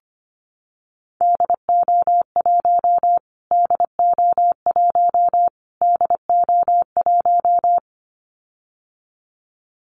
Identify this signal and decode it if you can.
Morse code, heard as a keyed tone, about 25 wpm, 700 Hz. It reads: DO1 DO1 DO1
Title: Callsign Prefixes for Top 25 DXCC - Repeated 3x - 25wpm